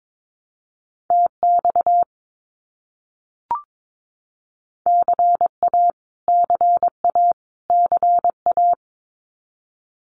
Morse code, heard as a keyed tone, about 22 wpm, 700 Hz. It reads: TX CA CA CA
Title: US States Repeated 3x 22wpm